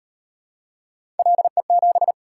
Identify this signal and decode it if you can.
Morse code, heard as a keyed tone, about 35 wpm, 700 Hz. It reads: LE7